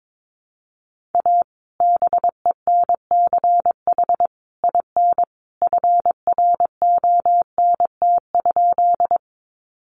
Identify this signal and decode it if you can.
Morse code, heard as a keyed tone, about 22 wpm, 700 Hz. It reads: A BENCH IN FRONT?
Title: Sentences from Top 1400 Words 22wpm